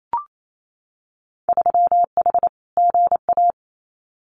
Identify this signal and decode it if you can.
Morse code, heard as a keyed tone, about 28 wpm, 700 Hz. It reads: 3H GA
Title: Winter Field Day Exchanges Encore 28wpm